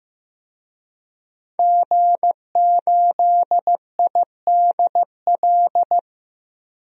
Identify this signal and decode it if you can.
Morse code, heard as a keyed tone, about 15 wpm, 700 Hz. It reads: G8IDL
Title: DXCC-100 Call Signs - 5 Characters - 15wpm